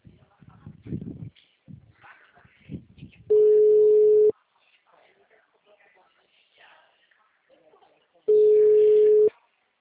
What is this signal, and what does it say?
Ringback tone (Europe, 425 Hz), with speech.
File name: out- -2011- - - .4489